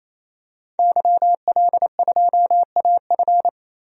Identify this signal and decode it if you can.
Morse code, heard as a keyed tone, about 28 wpm, 700 Hz. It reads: YL2AF
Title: DXCC-100 Call Signs - 5 Characters - 28wpm